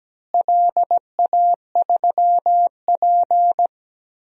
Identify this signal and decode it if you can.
Morse code, heard as a keyed tone, about 17 wpm, 700 Hz. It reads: LA3P
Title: DXCC-100 Call Signs - 4 Characters - Repeated 3x - 17wpm